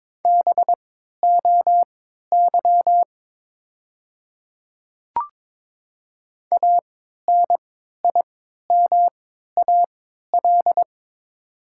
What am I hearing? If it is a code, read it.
Morse code, heard as a keyed tone, about 22 wpm, spaced as for 12 wpm, 700 Hz. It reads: BOY ANIMAL
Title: 3x Character Spacing - Top 300 Words - 22wpm